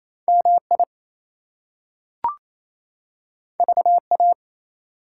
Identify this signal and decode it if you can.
Morse code, heard as a keyed tone, about 28 wpm, 700 Hz. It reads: MI VA
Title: US States and Canadian Provinces 28wpm